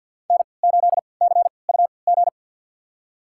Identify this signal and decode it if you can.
Morse code, heard as a keyed tone, about 48 wpm, spaced as for 28 wpm, 700 Hz. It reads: N8CUG
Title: USA Call Signs - 5 Characters - 3x Character Spacing - 50wpm